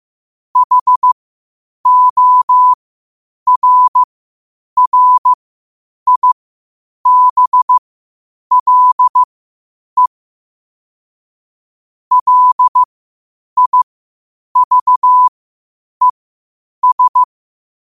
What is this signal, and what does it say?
Morse code, heard as a keyed tone, about 15 wpm, spaced as for 8 wpm, 1000 Hz. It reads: HORRIBLE LIVES
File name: qotdfiles / 8wpm